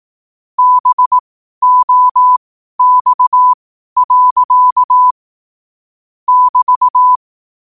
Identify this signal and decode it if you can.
Morse code, heard as a keyed tone, about 18 wpm, spaced as for 12 wpm, 1000 Hz. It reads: BOX. <BT>
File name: qotdfiles / 12wpm